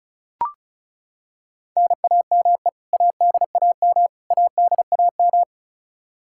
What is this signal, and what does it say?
Morse code, heard as a keyed tone, about 34 wpm, 700 Hz. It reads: NAME ADAM ADAM
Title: K1USN SST QSO Practice 35wpm